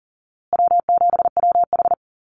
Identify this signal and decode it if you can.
Morse code, heard as a keyed tone, about 40 wpm, 700 Hz. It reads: W7WH
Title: USA Call Signs Repeated 2x 40wpm